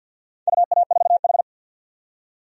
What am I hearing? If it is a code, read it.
Morse code, heard as a keyed tone, about 51 wpm, 700 Hz. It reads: UA4H
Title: DXCC-100 Call Signs - 4 Characters - 50wpm